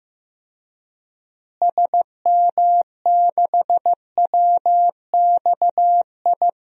Morse code, heard as a keyed tone, about 15 wpm, 700 Hz. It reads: SM6WXI
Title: DXCC-100 Call Signs - 6 Characters - Repeated 2x - 15wpm